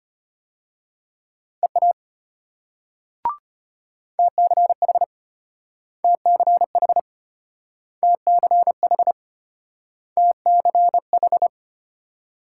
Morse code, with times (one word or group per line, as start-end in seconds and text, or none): 1.6-1.9 EA
4.2-5.0 TCH
6.0-7.0 TCH
8.0-9.1 TCH
10.2-11.5 TCH